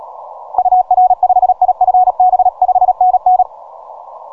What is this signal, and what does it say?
Morse code, heard as a keyed tone, about 37 wpm, 740 Hz. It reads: UR5IFB 5NN
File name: simple-file-list / VP8STI 12MCW